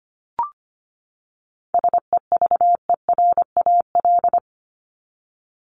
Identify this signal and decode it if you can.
Morse code, heard as a keyed tone, about 25 wpm, 700 Hz. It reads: SEVERAL
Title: Top 500 Words 25wpm